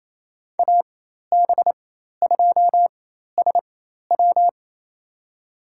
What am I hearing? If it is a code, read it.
Morse code, heard as a keyed tone, about 28 wpm, spaced as for 13 wpm, 700 Hz. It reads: AB2SW
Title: USA Call Signs - 5 Characters - 4x Character Spacing - 28wpm